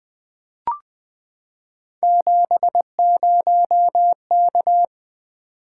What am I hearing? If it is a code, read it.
Morse code, heard as a keyed tone, about 20 wpm, 700 Hz. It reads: 70K
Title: Three Letter-Number Combinations - 20wpm